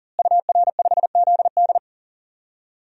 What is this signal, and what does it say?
Morse code, heard as a keyed tone, about 40 wpm, 700 Hz. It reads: UR5ZD
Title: DXCC-100 Call Signs - 5 Characters - Repeated 2x - 40wpm